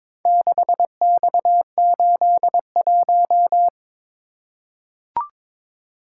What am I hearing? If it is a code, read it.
Morse code, heard as a keyed tone, about 22 wpm, 700 Hz. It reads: 6X81
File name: practice / 22wpm / Four Letter-Number Combinations - 22wpm